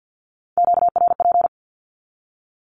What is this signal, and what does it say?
Morse code, heard as a keyed tone, about 50 wpm, 700 Hz. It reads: QRP
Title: Q-Codes 50wpm